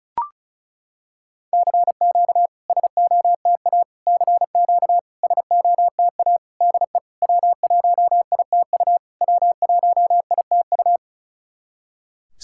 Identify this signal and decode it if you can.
Morse code, heard as a keyed tone, about 36 wpm, 700 Hz. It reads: CQ SOTA CQ SOTA DE W1ITU W1ITU
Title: SOTA QSO Practice 35wpm